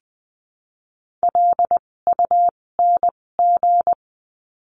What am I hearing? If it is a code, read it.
Morse code, heard as a keyed tone, about 20 wpm, spaced as for 16 wpm, 700 Hz. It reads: LUNG
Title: 256 Lesson 42a - Words 12-20wpm